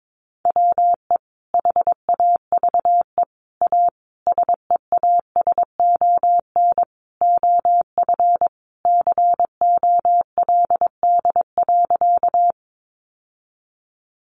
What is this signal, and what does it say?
Morse code, heard as a keyed tone, about 22 wpm, 700 Hz. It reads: WE HAVE A SEASON OF COLD.
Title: Sentences from Top 1400 Words 22wpm